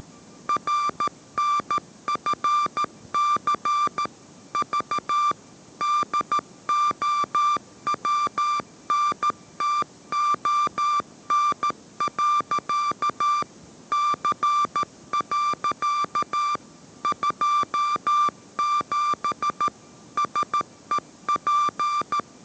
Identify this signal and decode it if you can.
Morse code, heard as a keyed tone, about 13 wpm, 1180 Hz. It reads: RNFC V DOWNTON. C. 27 SEP
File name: 14wpm